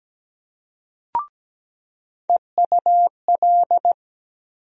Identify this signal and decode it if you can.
Morse code, heard as a keyed tone, about 17 wpm, 700 Hz. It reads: EUL
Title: USA Call Sign Postfixes - 3 Letters 17wpm